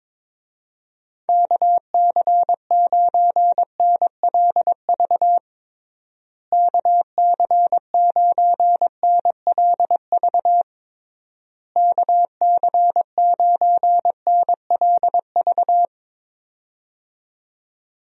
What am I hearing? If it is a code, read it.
Morse code, heard as a keyed tone, about 22 wpm, 700 Hz. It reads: KC9NLV KC9NLV KC9NLV
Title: USA Call Signs - 6 Characters - Repeated 3x - 22wpm